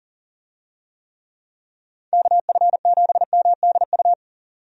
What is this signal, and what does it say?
Morse code, heard as a keyed tone, about 40 wpm, 700 Hz. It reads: KF7MDU
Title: USA Call Signs - 6 Characters - 40wpm